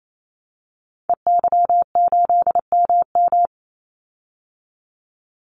Morse code, heard as a keyed tone, about 28 wpm, 700 Hz. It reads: EY8MM
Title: Call Signs Repeated 2x - 28wpm